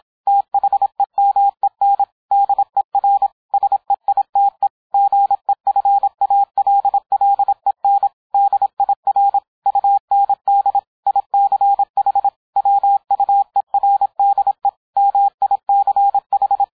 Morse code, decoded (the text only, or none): THEMEN DER SEITE GEFALLEN DIR UND ICH WUERDE MICH